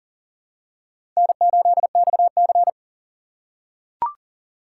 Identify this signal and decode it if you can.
Morse code, heard as a keyed tone, about 39 wpm, 700 Hz. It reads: N8XC
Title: USA Call Signs - 4 Characters - 3x Character Spacing - 40wpm